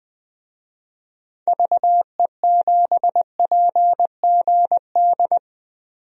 Morse code, heard as a keyed tone, about 20 wpm, 700 Hz. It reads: VE7PGD
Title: DXCC-100 Call Signs - 6 Characters - 20wpm